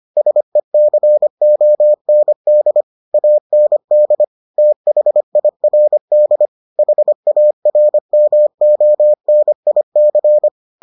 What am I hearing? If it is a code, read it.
Morse code, heard as a keyed tone, about 25 wpm, 600 Hz. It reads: SECOND AND THIRD HARMONIC